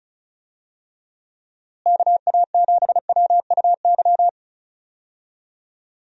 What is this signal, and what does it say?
Morse code, heard as a keyed tone, about 34 wpm, 700 Hz. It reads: KA7WUY
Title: USA Call Signs - 6 Characters - Repeated 2x - 35wpm